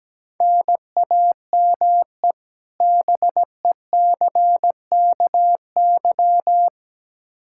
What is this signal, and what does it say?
Morse code, heard as a keyed tone, about 17 wpm, 700 Hz. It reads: NAME BECKY
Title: K1USN SST QSO Practice 17wpm